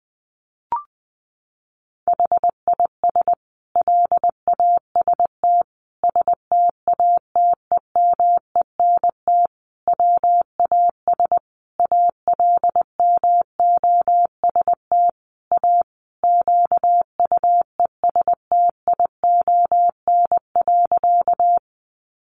Morse code, heard as a keyed tone, about 20 wpm, 700 Hz. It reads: HIS LAST STATEMENT WAS ALMOST A QUESTION.
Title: Sentences from Top 1200 Words 20wpm